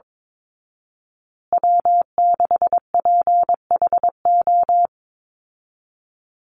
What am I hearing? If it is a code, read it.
Morse code, heard as a keyed tone, about 22 wpm, 700 Hz. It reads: W6PHO
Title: Call Signs Repeated 2x - 22wpm